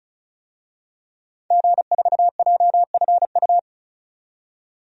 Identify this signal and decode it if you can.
Morse code, heard as a keyed tone, about 36 wpm, 700 Hz. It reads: G4JFU
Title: DXCC-100 Call Signs - 5 Characters - 35wpm